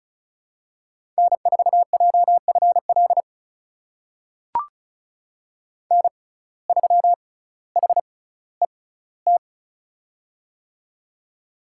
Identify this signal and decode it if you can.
Morse code, heard as a keyed tone, about 36 wpm, spaced as for 18 wpm, 700 Hz. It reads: N4JFL N3HET
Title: USA Call Signs - 5 Characters - 6x Character Spacing - 35wpm